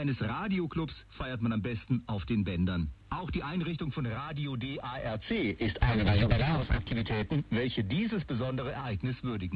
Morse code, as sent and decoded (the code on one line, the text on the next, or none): none